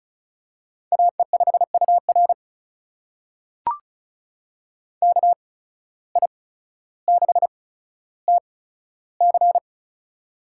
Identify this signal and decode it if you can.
Morse code, heard as a keyed tone, about 36 wpm, spaced as for 15 wpm, 700 Hz. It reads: AE5UR KI6TC
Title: USA Call Signs - 5 Characters - 8x Character Spacing - 35wpm